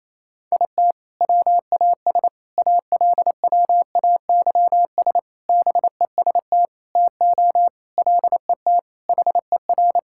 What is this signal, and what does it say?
Morse code, heard as a keyed tone, about 28 wpm, 700 Hz. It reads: IT WAS ALWAYS BEST TO LET HER
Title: Sentences from Top 1000 Words - Review Part 6 - 28wpm